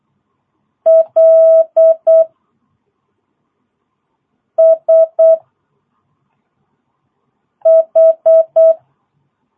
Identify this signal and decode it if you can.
Morse code, heard as a keyed tone, about 8 wpm, 660 Hz. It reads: L S H
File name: jh8cbh / 2022cw